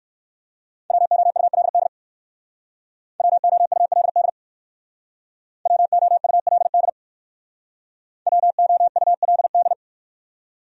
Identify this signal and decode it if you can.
Morse code, heard as a keyed tone, about 58 wpm, 700 Hz. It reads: WOULD WOULD WOULD WOULD